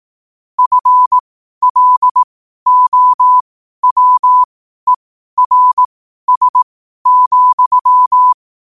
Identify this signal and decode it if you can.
Morse code, heard as a keyed tone, about 18 wpm, spaced as for 12 wpm, 1000 Hz. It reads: FLOWERS,
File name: qotdfiles / 12wpm